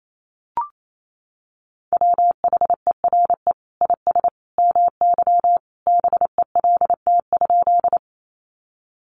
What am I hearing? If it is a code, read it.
Morse code, heard as a keyed tone, about 28 wpm, 700 Hz. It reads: WHERE IS MY BELT?
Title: Sentences from Top 1400 Words 28wpm